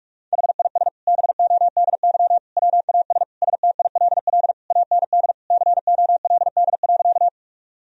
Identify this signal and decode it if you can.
Morse code, heard as a keyed tone, about 46 wpm, 700 Hz. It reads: HIS BODY WAS STILL AND COLD.
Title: Sentences from Top 1000 Words - Review Part 2 - 45wpm